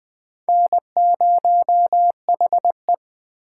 Morse code, heard as a keyed tone, about 20 wpm, 700 Hz. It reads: N0HE
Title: USA Call Signs - 4 Characters - 4x Character Spacing - 20wpm